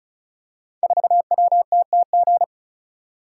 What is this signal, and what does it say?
Morse code, heard as a keyed tone, about 36 wpm, 700 Hz. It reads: 4WTTG
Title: Code Groups - 5 Letter:Numbers - 35wpm